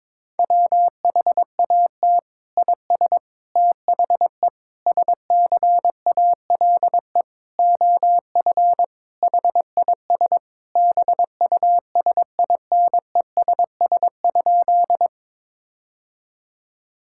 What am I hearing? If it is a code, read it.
Morse code, heard as a keyed tone, about 22 wpm, 700 Hz. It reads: WHAT IS THE SCALE OF HIS BUSINESS?
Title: Sentences from Top 1000 Words - Review Part 5 - 22wpm